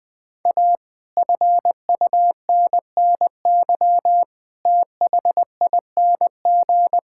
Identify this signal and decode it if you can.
Morse code, heard as a keyed tone, about 19 wpm, 700 Hz. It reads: A FUNNY THING
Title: Sentences with 9-Word Phrases - 20wpm